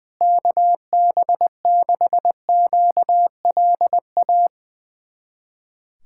Morse code, heard as a keyed tone, about 20 wpm, 700 Hz. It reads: KB6QLA
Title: USA Callsign DE Callsign 20wpm